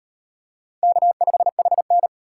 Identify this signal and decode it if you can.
Morse code, heard as a keyed tone, about 38 wpm, 700 Hz. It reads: K5HN